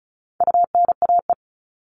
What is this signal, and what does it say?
Morse code, heard as a keyed tone, about 34 wpm, 700 Hz. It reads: UNAE
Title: All Cut Numbers - 4-Digit 35wpm